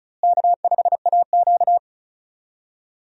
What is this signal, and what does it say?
Morse code, heard as a keyed tone, about 36 wpm, 700 Hz. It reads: K5AQ
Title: USA Call Signs - 4 Characters - Repeated 2x - 35wpm